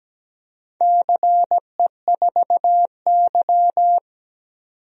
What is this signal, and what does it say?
Morse code, heard as a keyed tone, about 17 wpm, 700 Hz. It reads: CE4Y
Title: DXCC-100 Call Signs - 4 Characters - 17wpm